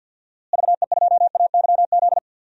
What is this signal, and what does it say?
Morse code, heard as a keyed tone, about 51 wpm, 700 Hz. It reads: VE2AYZ
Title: DXCC-100 Call Signs - 6 Characters - Repeated 3x - 50wpm